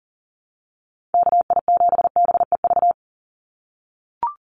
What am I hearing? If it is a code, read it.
Morse code, heard as a keyed tone, about 41 wpm, 700 Hz. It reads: KI7BEV
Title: USA Call Signs 40wpm